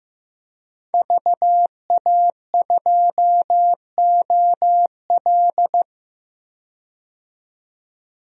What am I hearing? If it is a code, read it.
Morse code, heard as a keyed tone, about 15 wpm, 700 Hz. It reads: VA2OL
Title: DXCC-100 Call Signs - 5 Characters - 15wpm